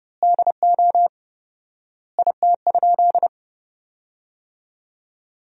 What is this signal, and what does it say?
Morse code, heard as a keyed tone, about 30 wpm, 700 Hz. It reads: DO IT?
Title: 4x Word-Spacing - Sentences from Top 200 Words - 30wpm